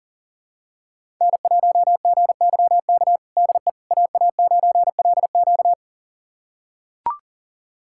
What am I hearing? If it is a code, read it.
Morse code, heard as a keyed tone, about 40 wpm, 700 Hz. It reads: N1GYK DE AA9LQ
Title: USA Callsign DE Callsign - 5 Characters - 40wpm